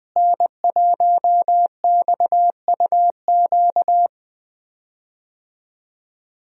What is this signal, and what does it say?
Morse code, heard as a keyed tone, about 20 wpm, 700 Hz. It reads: N1XUQ
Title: USA Call Signs - 5 Characters - Repeated 3x - 20wpm